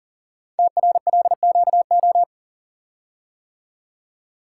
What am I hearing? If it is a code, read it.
Morse code, heard as a keyed tone, about 42 wpm, 690 Hz. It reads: TRLQO
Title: Code Groups - 5 Letter:Numbers - 40wpm